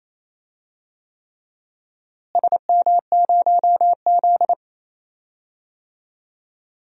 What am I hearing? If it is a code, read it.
Morse code, heard as a keyed tone, about 28 wpm, 700 Hz. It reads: SM0Z
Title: DXCC-100 Call Signs - 4 Characters - Repeated 3x - 28wpm